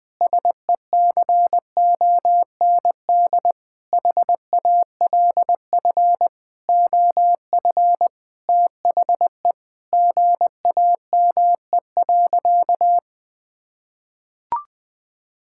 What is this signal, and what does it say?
Morse code, heard as a keyed tone, about 20 wpm, 700 Hz. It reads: SECOND HALF OF THE GAME.
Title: Sentences with 5-Word Phrases - 20wpm